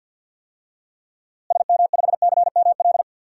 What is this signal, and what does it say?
Morse code, heard as a keyed tone, about 47 wpm, 700 Hz. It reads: SM5CML
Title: DXCC-100 Call Signs - 6 Characters - Repeated 3x - 50wpm